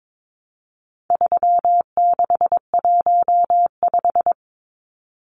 Morse code, heard as a keyed tone, about 22 wpm, 700 Hz. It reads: 3615
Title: Numbers - 4-Digit 22wpm